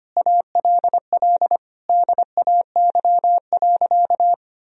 Morse code, heard as a keyed tone, about 25 wpm, 700 Hz. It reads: ALL DAY.
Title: Sentences from Top 1000 Words - Review Part 2 - 25wpm